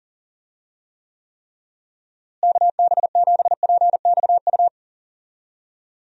Morse code, heard as a keyed tone, about 40 wpm, 700 Hz. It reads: KB7PXU
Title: USA Call Signs - 6 Characters - Repeated 3x - 40wpm